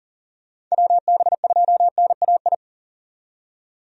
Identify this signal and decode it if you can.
Morse code, heard as a keyed tone, about 40 wpm, 700 Hz. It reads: WB2NAI
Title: USA Call Signs - 6 Characters - 40wpm